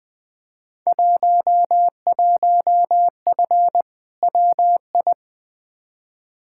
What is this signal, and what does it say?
Morse code, heard as a keyed tone, about 20 wpm, 700 Hz. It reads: 11F WI
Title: Field Day Exchanges Encore - 20wpm